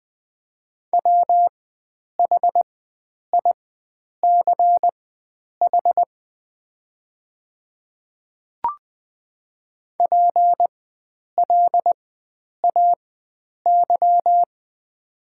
Morse code, heard as a keyed tone, about 20 wpm, spaced as for 9 wpm, 700 Hz. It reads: WHICH PLAY
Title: 4x Character Spacing - Top 500 Words - 20wpm